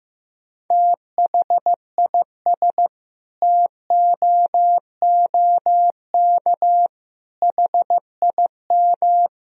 Morse code, heard as a keyed tone, about 15 wpm, 700 Hz. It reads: THIS TOOK HIM
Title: Sentences from Top 300 Words and Abbreviations 15wpm